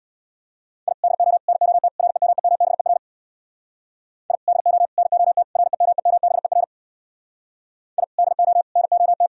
Morse code, heard as a keyed tone, about 64 wpm, 700 Hz. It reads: I DO NOT REMEMBER I DO NOT REMEMBER I DO NOT